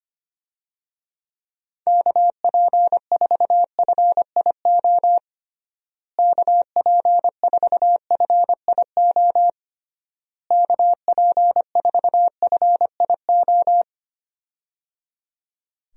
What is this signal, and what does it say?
Morse code, heard as a keyed tone, about 25 wpm, 700 Hz. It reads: KP4FIO KP4FIO KP4FIO
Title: USA Call Signs - 6 Characters - Repeated 3x - 25wpm